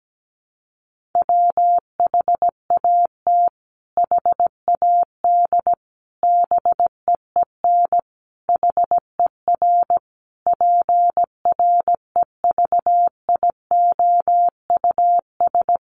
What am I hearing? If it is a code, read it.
Morse code, heard as a keyed tone, about 17 wpm, 700 Hz. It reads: WHAT HAD BEEN HER PREVIOUS
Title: Sentences from Top 1500 Words 17wpm